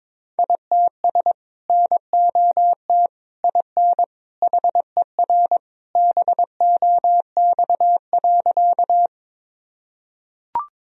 Morse code, heard as a keyed tone, about 22 wpm, 700 Hz. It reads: ITS NOT IN HER BOX.
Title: Sentences from Top 1000 Words - Review Part 2 - 22wpm